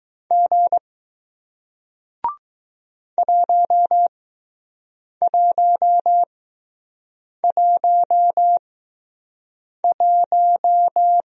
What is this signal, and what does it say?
Morse code, heard as a keyed tone, about 20 wpm, spaced as for 6 wpm, 700 Hz. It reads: G 1111